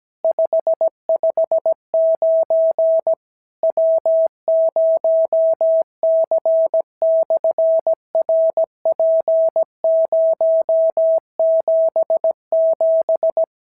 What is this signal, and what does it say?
Morse code, heard as a keyed tone, about 17 wpm, 650 Hz. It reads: 559 W0C/RP077
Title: SOTA QSO Practice 17wpm